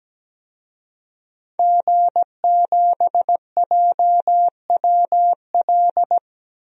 Morse code, heard as a keyed tone, about 17 wpm, 700 Hz. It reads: G7JWL
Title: DXCC-100 Call Signs - 5 Characters - Repeated 3x - 17wpm